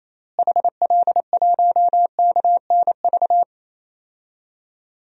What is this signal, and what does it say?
Morse code, heard as a keyed tone, about 28 wpm, 700 Hz. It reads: HL1KNV
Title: DXCC-100 Call Signs - 6 Characters - Repeated 3x - 28wpm